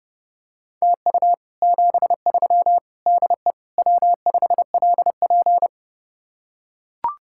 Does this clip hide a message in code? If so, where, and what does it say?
Yes: Morse code from 0.8 to 5.7 s, heard as a keyed tone, about 29 wpm, 700 Hz. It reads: TU 73 DE W5LP